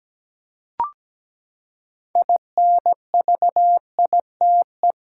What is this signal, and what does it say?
Morse code, heard as a keyed tone, about 17 wpm, 700 Hz. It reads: INVITE
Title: Six Letter Words - 17wpm